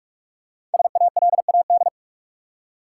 Morse code, heard as a keyed tone, about 44 wpm, 700 Hz. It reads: SALAD
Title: Five Letter Words - 45wpm